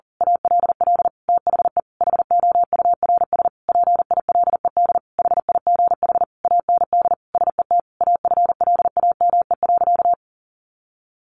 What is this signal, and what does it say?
Morse code, heard as a keyed tone, about 40 wpm, 700 Hz. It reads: ALL THE HOURS PILED HIGH AND SET AFLAME.
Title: A Sound Of Thunder 40wpm